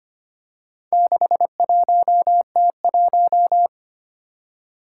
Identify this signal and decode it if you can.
Morse code, heard as a keyed tone, about 25 wpm, 700 Hz. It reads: 61T1
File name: practice / 25wpm / Four Letter-Number Combinations - 25wpm